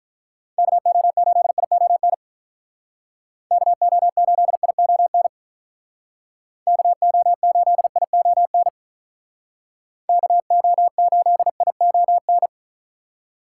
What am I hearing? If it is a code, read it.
Morse code, heard as a keyed tone, about 45 wpm, 700 Hz. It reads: KO8ION KO8ION KO8ION KO8ION